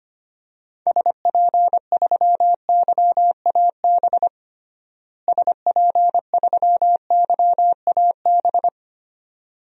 Morse code, heard as a keyed tone, about 25 wpm, 700 Hz. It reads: SP3YAB SP3YAB
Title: DXCC-100 Call Signs - 6 Characters - Repeated 2x - 25wpm